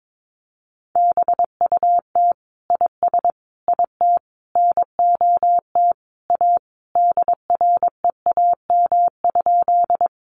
Morse code, heard as a keyed tone, about 22 wpm, 700 Hz. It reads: BUT IS IT NOT A DREAM?
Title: Sentences from Top 1500 Words 22wpm